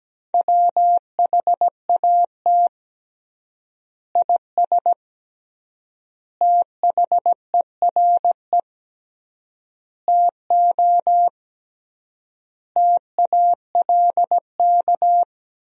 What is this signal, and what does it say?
Morse code, heard as a keyed tone, about 17 wpm, 700 Hz. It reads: WHAT IS THERE TO TALK
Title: 3x Word-Spacing - Sentences from Top 400 Words - 17wpm